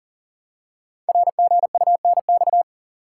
Morse code, heard as a keyed tone, about 42 wpm, 700 Hz. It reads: RGUNX
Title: Code Groups - 5 Letters - 40wpm